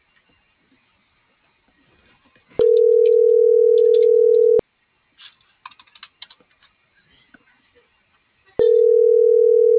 Ringback tone (North America), 440 + 480 Hz.